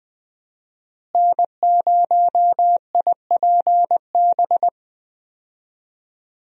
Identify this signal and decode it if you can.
Morse code, heard as a keyed tone, about 20 wpm, 700 Hz. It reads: N0IPB
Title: USA Call Signs - 5 Characters - Repeated 2x - 20wpm